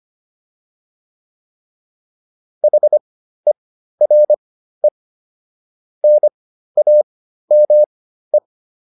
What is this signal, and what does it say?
Morse code, heard as a keyed tone, about 25 wpm, spaced as for 13 wpm, 600 Hz. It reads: HERE NAME